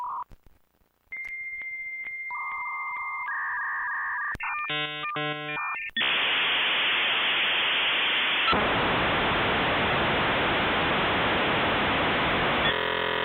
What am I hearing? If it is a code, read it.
Dial-up modem handshake.